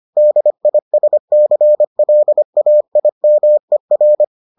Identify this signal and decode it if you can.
Morse code, heard as a keyed tone, about 25 wpm, 600 Hz. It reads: DISCLAIMER